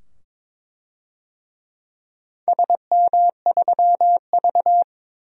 Morse code, heard as a keyed tone, about 21 wpm, 700 Hz. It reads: SM3V
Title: DXCC-100 Call Signs - 4 Characters - 22wpm